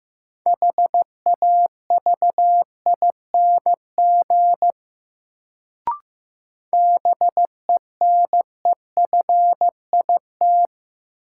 Morse code, heard as a keyed tone, about 15 wpm, 700 Hz. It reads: HAVING BENEFIT
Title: Sets of 5 Words Encore - New Word Drill - 15wpm